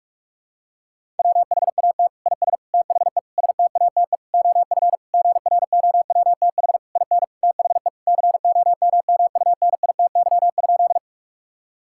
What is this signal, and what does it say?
Morse code, heard as a keyed tone, about 41 wpm, 700 Hz. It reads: WHAT IS THE STATE OF GROWTH IN THE COMMUNITY?
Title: Sentences from Top 1000 Words - Review Part 5 - 45wpm